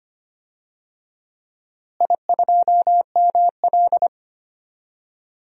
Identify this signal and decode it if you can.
Morse code, heard as a keyed tone, about 25 wpm, 700 Hz. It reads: I2ML
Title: DXCC-100 Call Signs - 4 Characters - Repeated 2x - 25wpm